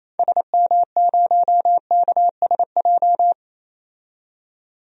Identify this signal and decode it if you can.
Morse code, heard as a keyed tone, about 28 wpm, 700 Hz. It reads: SM0KSJ
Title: DXCC-100 Call Signs - 6 Characters - 28wpm